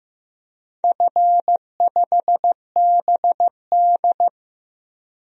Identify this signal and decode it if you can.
Morse code, heard as a keyed tone, about 15 wpm, 700 Hz. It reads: F5BD
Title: DXCC-100 Call Signs - 4 Characters - Repeated 3x - 15wpm